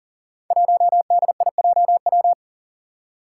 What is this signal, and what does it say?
Morse code, heard as a keyed tone, about 42 wpm, 700 Hz. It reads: 1DIJW
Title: Five Challenging Letter-Number Combinations 40wpm